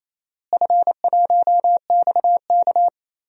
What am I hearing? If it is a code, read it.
Morse code, heard as a keyed tone, about 28 wpm, 700 Hz. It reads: F1XK
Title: DXCC-100 Call Signs - 4 Characters - Repeated 2x - 28wpm